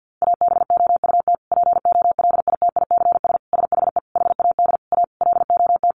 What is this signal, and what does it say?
Morse code, heard as a keyed tone, about 50 wpm, 700 Hz. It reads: ABOUT POLITICS SHE HAD A LOT
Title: Sentences from Top 1000 Words 50wpm